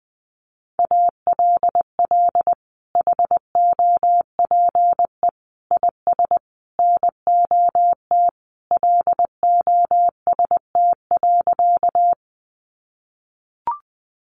Morse code, heard as a keyed tone, about 20 wpm, 700 Hz. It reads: ALL HOPE IS NOT LOST.
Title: Sentences from Top 1400 Words 20wpm